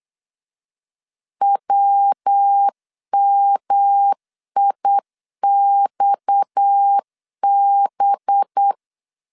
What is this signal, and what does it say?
Morse code, heard as a keyed tone, about 8 wpm, 800 Hz. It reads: WMIXB
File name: 8wpm